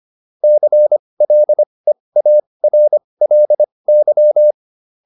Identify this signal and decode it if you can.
Morse code, heard as a keyed tone, about 25 wpm, spaced as for 20 wpm, 600 Hz. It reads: CLEARLY